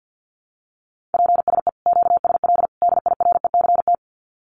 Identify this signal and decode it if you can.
Morse code, heard as a keyed tone, about 50 wpm, 700 Hz. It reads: PSE QSL DIRECT
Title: Common QSO Phrases 50wpm